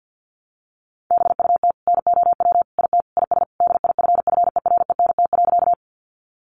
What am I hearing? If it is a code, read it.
Morse code, heard as a keyed tone, about 50 wpm, 700 Hz. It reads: BUT NOW IT IS DIFFERENT.
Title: Sentences from Top 600 Words 50wpm